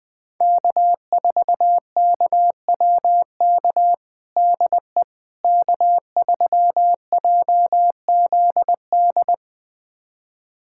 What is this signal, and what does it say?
Morse code, heard as a keyed tone, about 20 wpm, 700 Hz. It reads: K4KWK DE K3JZD
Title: USA Callsign DE Callsign - 5 Characters - 20wpm